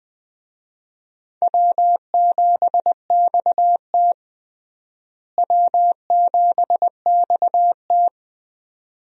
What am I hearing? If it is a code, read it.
Morse code, heard as a keyed tone, about 20 wpm, 700 Hz. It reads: W7XT W7XT
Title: USA Call Signs - 4 Characters - Repeated 2x - 20wpm